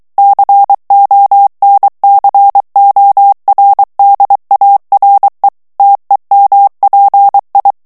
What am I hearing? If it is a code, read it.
Morse code, heard as a keyed tone, about 23 wpm, 800 Hz. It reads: CONCORDARE TEMPI
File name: morse mp3 / ITALIEN texte clair 1 vitesse 22 mots minute